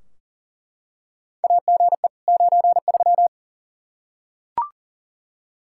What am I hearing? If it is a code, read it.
Morse code, heard as a keyed tone, about 37 wpm, 700 Hz. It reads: AGE 93
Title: Practice QSO - Exchange 3 - More Info - 40wpm